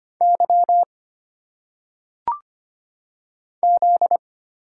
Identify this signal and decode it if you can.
Morse code, heard as a keyed tone, about 25 wpm, 700 Hz. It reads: Y Z
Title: Single Letters - Phonetic Pronunciation 25wpm